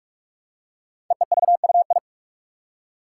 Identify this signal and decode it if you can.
Morse code, heard as a keyed tone, about 44 wpm, 700 Hz. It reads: EEVUI
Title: Code Groups - 5 Letters - 45wpm